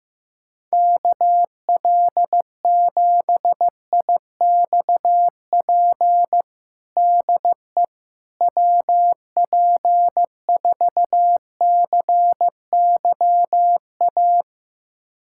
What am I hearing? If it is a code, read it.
Morse code, heard as a keyed tone, about 15 wpm, 700 Hz. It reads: KL7IXP DE WP4CYA
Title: USA Callsign DE Callsign - 6 Characters - 15wpm